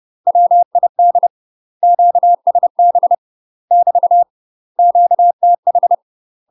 Morse code, heard as a keyed tone, about 30 wpm, 700 Hz. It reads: WID QSB <BT> QTH
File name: UK MASTER QSOs TRAINING / 30WPM / UK MASTER QSOs 0035